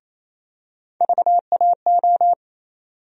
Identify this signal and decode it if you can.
Morse code, heard as a keyed tone, about 28 wpm, 700 Hz. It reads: VAO
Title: USA Call Sign Postfixes - 3 Letters 28wpm